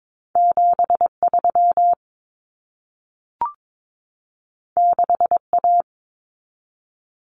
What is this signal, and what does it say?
Morse code, heard as a keyed tone, about 22 wpm, 700 Hz. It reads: 73 6A
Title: Common Cut Numbers - 2-Digit 22wpm